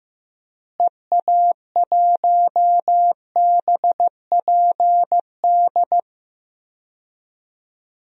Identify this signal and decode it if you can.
Morse code, heard as a keyed tone, about 15 wpm, 700 Hz. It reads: EA1BPD
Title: DXCC-100 Call Signs - 6 Characters - Repeated 2x - 15wpm